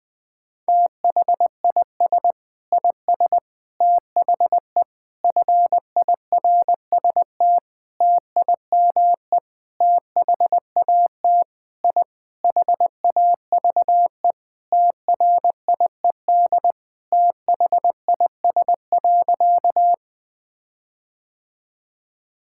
Morse code, heard as a keyed tone, about 20 wpm, 700 Hz. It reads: THIS IS THE FIRST TIME THAT I HAVE TRIED THIS.
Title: Sentences with 6-Word Phrases - 20wpm